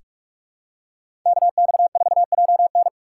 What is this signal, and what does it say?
Morse code, heard as a keyed tone, about 46 wpm, 700 Hz. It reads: KX4JN
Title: USA Call Signs - 5 Characters - 6x Character Spacing - 45wpm